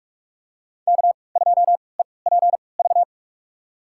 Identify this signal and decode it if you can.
Morse code, heard as a keyed tone, about 45 wpm, spaced as for 25 wpm, 700 Hz. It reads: K2EPV
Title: USA Call Signs - 5 Characters - 3x Character Spacing - 45wpm